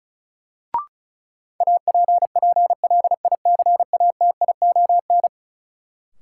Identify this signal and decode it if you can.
Morse code, heard as a keyed tone, about 36 wpm, 700 Hz. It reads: APPLICATION
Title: Sentences from Top 1200 Words - New Word Drill Alt - 35wpm